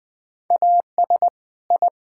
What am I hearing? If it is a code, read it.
Morse code, heard as a keyed tone, about 21 wpm, 700 Hz. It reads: AS I
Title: Build A Phrase - 2-Word Phrases - 20wpm